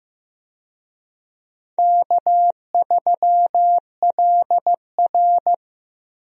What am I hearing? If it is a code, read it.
Morse code, heard as a keyed tone, about 15 wpm, 700 Hz. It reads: K3LR
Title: USA Call Signs - 4 Characters - Repeated 3x - 15wpm